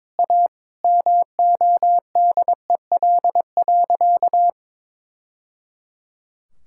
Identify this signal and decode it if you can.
Morse code, heard as a keyed tone, about 22 wpm, 700 Hz. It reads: A MODEL.
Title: Sentences from Top 1000 Words - Review Part 5 - 22wpm